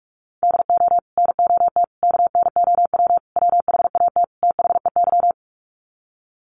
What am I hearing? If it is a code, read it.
Morse code, heard as a keyed tone, about 44 wpm, 700 Hz. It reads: DO NOT KNOW WHAT THEY
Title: Sets of 5 Words 45wpm